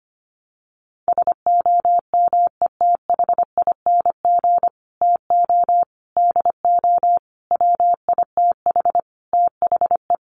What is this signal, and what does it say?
Morse code, heard as a keyed tone, about 25 wpm, 700 Hz. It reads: SOMETHING TO DO WITH THE
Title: Sets of 5 Words Encore 25wpm